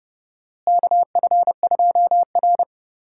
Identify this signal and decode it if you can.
Morse code, heard as a keyed tone, about 31 wpm, 700 Hz. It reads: KF2R
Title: USA Call Signs - 4 Characters - Repeated 2x - 30wpm